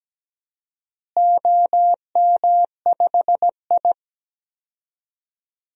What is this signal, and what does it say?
Morse code, heard as a keyed tone, about 17 wpm, 700 Hz. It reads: OM5I
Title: DXCC-100 Call Signs - 4 Characters - Repeated 3x - 17wpm